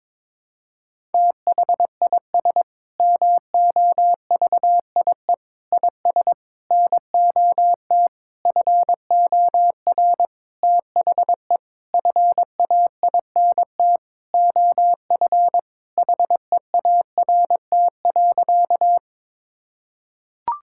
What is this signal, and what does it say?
Morse code, heard as a keyed tone, about 22 wpm, 700 Hz. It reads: THIS MOVIE IS NOT FOR THE FAINT OF HEART.
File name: Sentences with 7-Word Phrases - 22wpm